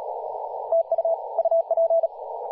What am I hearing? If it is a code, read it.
Morse code, heard as a keyed tone, about 38 wpm, 660 Hz, with radio noise. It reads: TU UP
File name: posts / 2016-july-20-0522z-t2cow-18-071mhz-cw